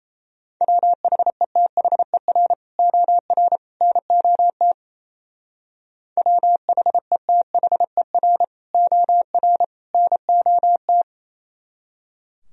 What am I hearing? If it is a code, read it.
Morse code, heard as a keyed tone, about 34 wpm, 700 Hz. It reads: WHETHER OR NOT WHETHER OR NOT
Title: Speed-Racing - Sets of 3 Words 28wpm